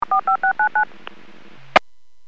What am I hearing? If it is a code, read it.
DTMF, heard as telephone touch tones, tones of about 110 ms. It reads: 45699
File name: WPS / 00667 18-02 WPS 451.25 WPS